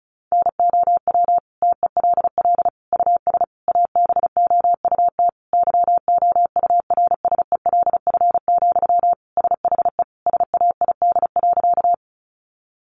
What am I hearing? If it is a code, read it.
Morse code, heard as a keyed tone, about 34 wpm, 700 Hz. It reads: NOW TELL US ABOUT YOURSELF, SHE SAID.
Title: Sentences from Top 600 Words 35wpm